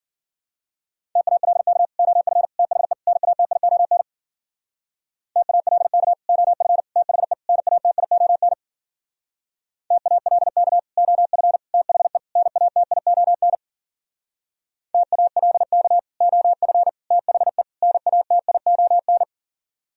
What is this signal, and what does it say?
Morse code, heard as a keyed tone, about 53 wpm, 700 Hz. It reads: TALK OF THE NATION TALK OF THE NATION TALK OF THE NATION TALK OF THE NATION